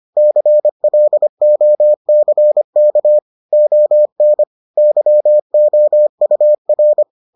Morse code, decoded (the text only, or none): CLOCK ON YOUR